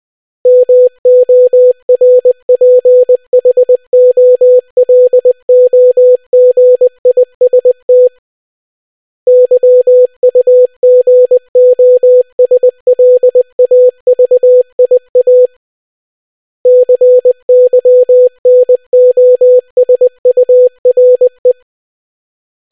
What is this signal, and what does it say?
Morse code, heard as a keyed tone, about 20 wpm, 500 Hz. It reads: MORPHOLOGIST YUGOSLAVIA CYNOSURE